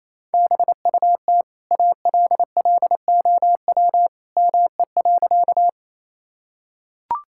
Morse code, heard as a keyed tone, about 28 wpm, 700 Hz. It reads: BUT ALLOW ME.
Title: Sentences from Top 1000 Words - Review Part 5 - 28wpm